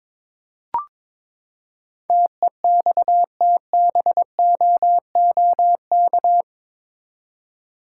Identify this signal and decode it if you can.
Morse code, heard as a keyed tone, about 21 wpm, 700 Hz. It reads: TEXTBOOK
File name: practice / 22wpm / Eight Letter Words - 22wpm